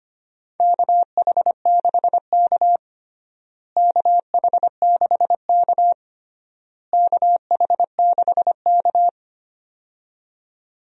Morse code, heard as a keyed tone, about 25 wpm, 700 Hz. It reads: KH6K KH6K KH6K
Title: DXCC-100 Call Signs - 4 Characters - Repeated 3x - 25wpm